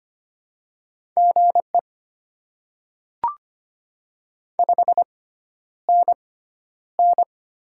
Morse code, heard as a keyed tone, about 25 wpm, spaced as for 11 wpm, 700 Hz. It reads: GE 5NN
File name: Top 20 QSO Elements - 6x Character Spacing - 25wpm